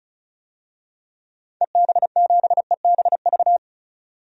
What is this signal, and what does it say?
Morse code, heard as a keyed tone, about 34 wpm, 700 Hz. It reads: EB7EBV
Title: DXCC-100 Call Signs - 6 Characters - 35wpm